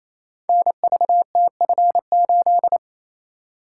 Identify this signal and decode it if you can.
Morse code, heard as a keyed tone, about 28 wpm, 690 Hz. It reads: NVTF8
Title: Code Groups - 5 Letter:Numbers - 28wpm